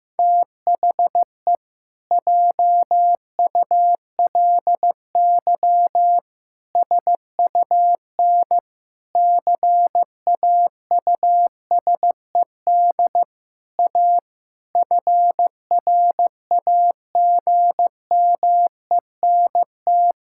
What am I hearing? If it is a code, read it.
Morse code, heard as a keyed tone, about 16 wpm, 700 Hz. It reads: THE JULY SUN CAUSED A FRAGMENT
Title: Pangram Challenge v2 - 15wpm